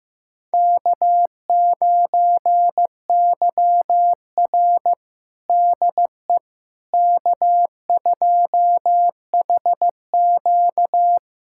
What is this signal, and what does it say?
Morse code, heard as a keyed tone, about 15 wpm, 700 Hz. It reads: K9YR DE K2HQ
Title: USA Callsign DE Callsign - 4 Characters - 15wpm